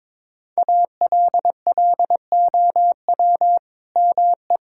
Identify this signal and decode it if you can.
Morse code, heard as a keyed tone, about 21 wpm, 700 Hz. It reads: ALLOW ME
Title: Sentences from Top 1000 Words - Review Part 5 - 22wpm